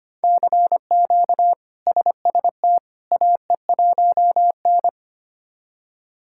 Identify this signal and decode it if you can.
Morse code, heard as a keyed tone, about 25 wpm, 700 Hz. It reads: CQ SST AE1N
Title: K1USN SST QSO Practice - 25wpm